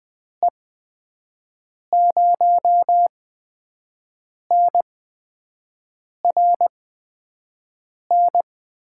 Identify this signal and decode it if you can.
Morse code, heard as a keyed tone, about 20 wpm, spaced as for 5 wpm, 700 Hz. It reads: E0NRN
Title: USA Call Signs - 6 Characters - 8x Character Spacing - 20wpm